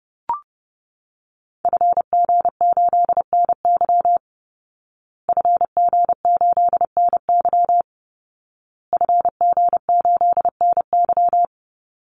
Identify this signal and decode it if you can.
Morse code, heard as a keyed tone, about 30 wpm, 700 Hz. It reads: FG8NY FG8NY FG8NY
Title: Call Signs Repeated 2x - 30wpm